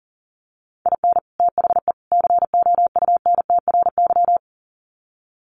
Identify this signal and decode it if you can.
Morse code, heard as a keyed tone, about 40 wpm, 700 Hz. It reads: IN THE COUNTRY
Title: Sets of 3 Words 40wpm